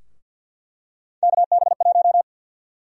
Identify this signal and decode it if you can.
Morse code, heard as a keyed tone, about 48 wpm, 700 Hz. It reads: KB1
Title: USA Call Sign Prefixes - 50wpm